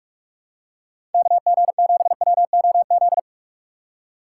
Morse code, heard as a keyed tone, about 44 wpm, 700 Hz. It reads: KG7WOZ
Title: USA Call Signs - 6 Characters - 3x Character Spacing - 45wpm